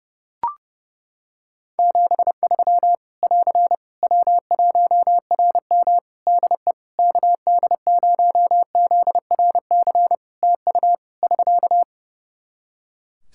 Morse code, heard as a keyed tone, about 30 wpm, 700 Hz. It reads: 73 <AR> W1RM DE KD0ZRC TU <SK>
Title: Practice QSO - Exchange 4 - Ending - 30wpm